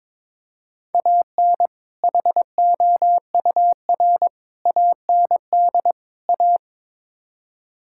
Morse code, heard as a keyed tone, about 22 wpm, 700 Hz. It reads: AN HOUR AND A
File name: practice / 22wpm / Build A Phrase - 5-Word Phrases - 22wpm